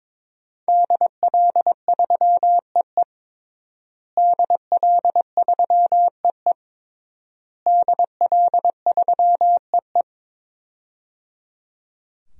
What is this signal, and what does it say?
Morse code, heard as a keyed tone, about 22 wpm, 700 Hz. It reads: DL3EE DL3EE DL3EE
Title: DXCC-100 Call Signs - 5 Characters - Repeated 3x - 22wpm